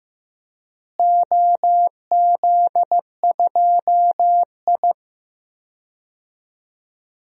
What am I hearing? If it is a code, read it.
Morse code, heard as a keyed tone, about 15 wpm, 700 Hz. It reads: OZ2I
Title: DXCC-100 Call Signs - 4 Characters - Repeated 3x - 15wpm